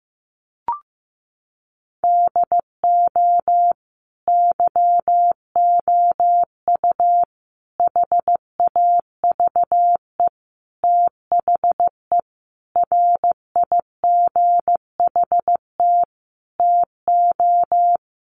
Morse code, heard as a keyed tone, about 15 wpm, 700 Hz. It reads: DO YOU HAVE THE RIGHT TO
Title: Sentences from Top 1400 Words 15wpm